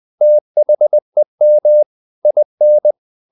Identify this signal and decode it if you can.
Morse code, heard as a keyed tone, about 20 wpm, 600 Hz. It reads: THEM IN
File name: the old man and the sea / the old man and the sea part 1 20wpm